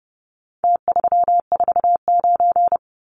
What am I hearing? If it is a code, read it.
Morse code, heard as a keyed tone, about 30 wpm, 700 Hz. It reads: T349
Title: Numbers - Contest Style - 4-Digit 30wpm